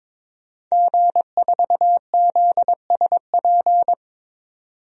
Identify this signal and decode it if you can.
Morse code, heard as a keyed tone, about 22 wpm, 700 Hz. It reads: G4ZSP
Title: DXCC-100 Call Signs - 5 Characters - 22wpm